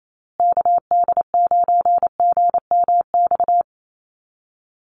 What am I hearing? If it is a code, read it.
Morse code, heard as a keyed tone, about 28 wpm, 700 Hz. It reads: KD9GMX
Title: USA Call Signs Repeated 3x 28wpm